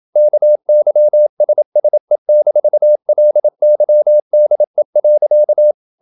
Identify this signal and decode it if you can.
Morse code, heard as a keyed tone, about 27 wpm, 600 Hz. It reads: KYSSE-LYDE.
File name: Ferie QRQ 27wpm